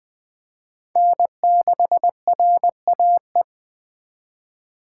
Morse code, heard as a keyed tone, about 20 wpm, 700 Hz. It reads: N6RAE